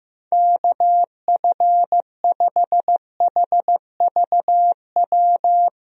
Morse code, heard as a keyed tone, about 15 wpm, 700 Hz. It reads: KF5HVW
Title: USA Call Signs - 6 Characters - Repeated 3x - 15wpm